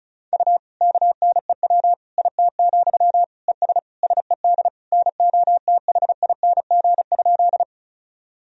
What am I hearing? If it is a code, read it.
Morse code, heard as a keyed tone, about 36 wpm, 700 Hz. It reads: U KNEW IT, ES SED NOTHING?
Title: Sentences from Top 500 Words and Abbreviations 35wpm